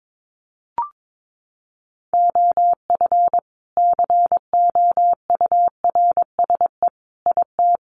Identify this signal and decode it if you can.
Morse code, heard as a keyed tone, about 22 wpm, 700 Hz. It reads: OF COURSE IT
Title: Sentences from Top 500 Words 22wpm